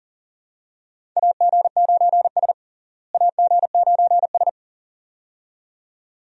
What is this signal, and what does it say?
Morse code, heard as a keyed tone, about 40 wpm, 700 Hz. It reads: AG9S AG9S
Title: USA Call Signs - 4 Characters - Repeated 2x - 40wpm